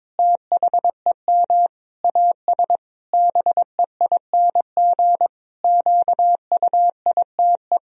Morse code, heard as a keyed tone, about 22 wpm, 700 Hz. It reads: THEM AS BEING QUITE
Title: Sentences from Top 1000 Words - Review Part 3 - 22wpm